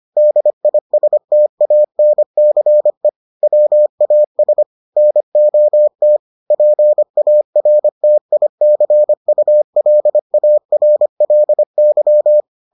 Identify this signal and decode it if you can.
Morse code, heard as a keyed tone, about 25 wpm, 600 Hz. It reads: DISTANCE WAS NOT PARTICULARLY